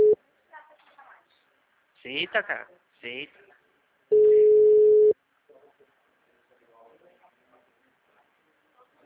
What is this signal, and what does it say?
Ringback tone (Europe, 425 Hz), with speech.